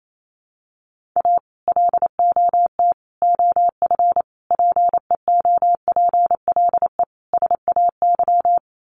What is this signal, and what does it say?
Morse code, heard as a keyed tone, about 28 wpm, 700 Hz. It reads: A LOT OF PEOPLE SAY
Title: Sets of 5 Words Encore2 28wpm